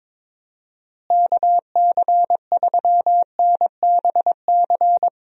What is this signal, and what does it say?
Morse code, heard as a keyed tone, about 22 wpm, 700 Hz. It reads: KC3NBC
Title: USA Call Signs - 6 Characters - Repeated 2x - 22wpm